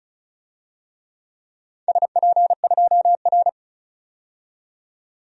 Morse code, heard as a keyed tone, about 34 wpm, 700 Hz. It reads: SP2R
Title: DXCC-100 Call Signs - 4 Characters - 35wpm